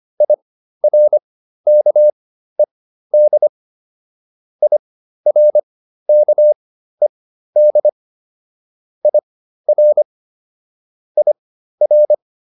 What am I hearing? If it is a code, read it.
Morse code, heard as a keyed tone, about 25 wpm, spaced as for 13 wpm, 600 Hz. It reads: IRKED IRKED IR IR